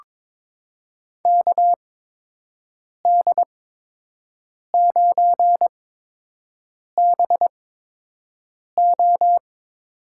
Morse code, heard as a keyed tone, about 23 wpm, spaced as for 6 wpm, 700 Hz. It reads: KD9BO
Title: USA Call Signs - 6 Characters - 8x Character Spacing - 22wpm